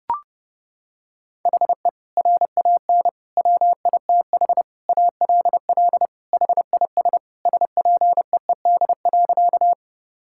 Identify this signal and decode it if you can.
Morse code, heard as a keyed tone, about 30 wpm, 700 Hz. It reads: HE RAN WITH ALL HIS SPEED.
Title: Sentences from Top 1000 Words - Review Part 5 - 30wpm